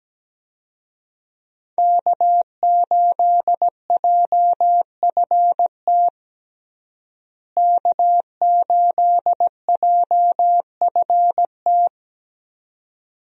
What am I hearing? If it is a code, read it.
Morse code, heard as a keyed tone, about 17 wpm, 700 Hz. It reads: K8JFT K8JFT
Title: USA Call Signs - 5 Characters - Repeated 2x - 17wpm